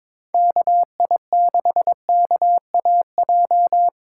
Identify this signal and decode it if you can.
Morse code, heard as a keyed tone, about 22 wpm, 700 Hz. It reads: KI6KAJ
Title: USA Call Signs - 6 Characters - 4x Character Spacing - 22wpm